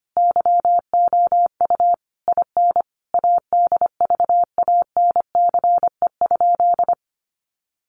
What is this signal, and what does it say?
Morse code, heard as a keyed tone, about 25 wpm, 700 Hz. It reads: YOU IN ADVANCE?
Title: Sentences from Top 1400 Words 25wpm